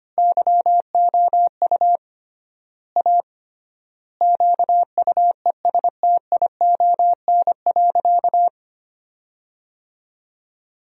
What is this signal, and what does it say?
Morse code, heard as a keyed tone, about 25 wpm, 700 Hz. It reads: YOU A QUESTION.
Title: 3x Word-Spacing - Sentences from Top 400 Words - 25wpm